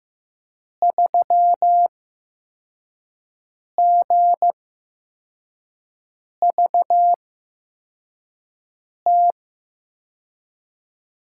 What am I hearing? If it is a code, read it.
Morse code, heard as a keyed tone, about 15 wpm, spaced as for 4 wpm, 700 Hz. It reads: 3GVT
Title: USA Call Signs - 8x Character Spacing - 15wpm